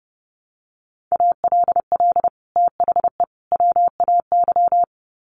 Morse code, heard as a keyed tone, about 30 wpm, 700 Hz. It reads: ALL THE WAY
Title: Sets of 3 Words 30wpm